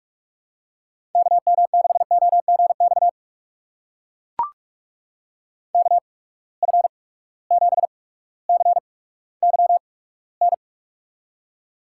Morse code, heard as a keyed tone, about 44 wpm, spaced as for 19 wpm, 700 Hz. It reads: KM6OGX KF7CYN
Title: USA Call Signs - 6 Characters - 8x Character Spacing - 45wpm